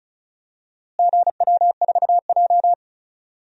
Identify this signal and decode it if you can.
Morse code, heard as a keyed tone, about 34 wpm, 700 Hz. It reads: GW4J
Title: DXCC-100 Call Signs - 4 Characters - Repeated 2x - 35wpm